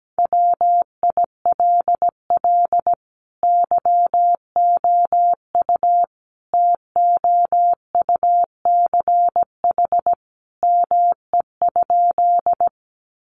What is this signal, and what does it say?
Morse code, heard as a keyed tone, about 17 wpm, 700 Hz. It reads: WILL YOU TOUCH ME?
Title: Sentences from Top 1200 Words 17wpm